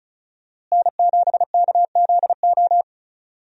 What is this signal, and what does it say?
Morse code, heard as a keyed tone, about 36 wpm, 700 Hz. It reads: N7KZO
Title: USA Call Signs - 5 Characters - Repeated 3x - 35wpm